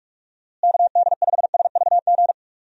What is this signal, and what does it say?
Morse code, heard as a keyed tone, about 44 wpm, 700 Hz. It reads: KD5SVG
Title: USA Call Signs - 6 Characters - Repeated 2x - 45wpm